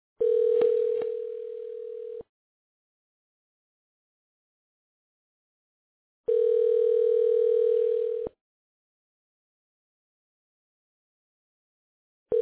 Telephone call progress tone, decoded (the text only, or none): ringback tone